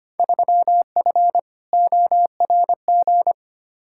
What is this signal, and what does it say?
Morse code, heard as a keyed tone, about 25 wpm, 700 Hz. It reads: 3F ORG
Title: Field Day Exchanges - 25wpm